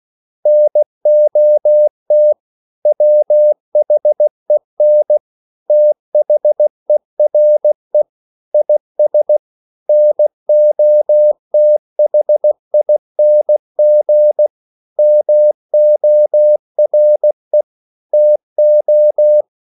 Morse code, heard as a keyed tone, about 16 wpm, 600 Hz. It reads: NOT WHEN THERE IS NOTHING MORE TO
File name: CW / Practice / EnglishQuotations0216